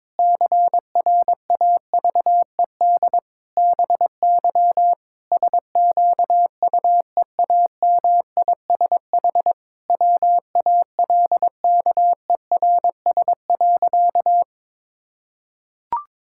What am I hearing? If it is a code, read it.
Morse code, heard as a keyed tone, about 22 wpm, 700 Hz. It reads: CRAVED BY SQUEAMISH WALKERS.
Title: Pangram Challenge v2 - 22wpm